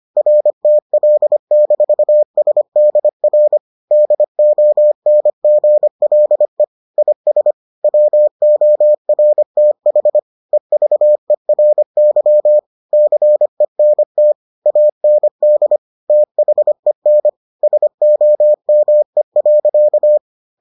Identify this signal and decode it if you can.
Morse code, heard as a keyed tone, about 25 wpm, 600 Hz. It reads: RTL-SDR DONGLE IS WORTH EVERY CENT AND THEN SOME.